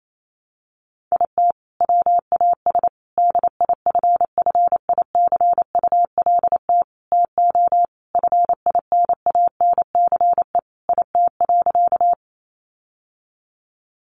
Morse code, heard as a keyed tone, about 28 wpm, 700 Hz. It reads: IT WAS DIFFICULT TO FINANCE IT.
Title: Sentences from Top 1500 Words 28wpm